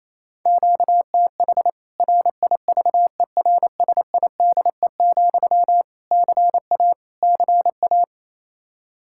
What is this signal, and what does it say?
Morse code, heard as a keyed tone, about 28 wpm, 700 Hz. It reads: QTH RIVERSIDE, CA CA
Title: K1USN SST QSO Practice 28wpm